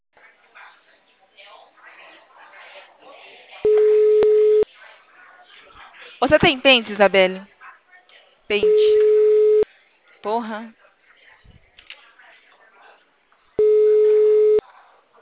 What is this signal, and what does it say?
Ringback tone (Europe, 425 Hz), with speech.